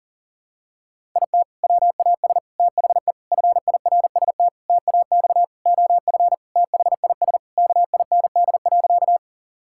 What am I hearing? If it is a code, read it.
Morse code, heard as a keyed tone, about 40 wpm, 700 Hz. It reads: IT WAS THE FIRST TAX OF THIS KIND.
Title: Sentences from Top 1000 Words - Review Part 5 - 40wpm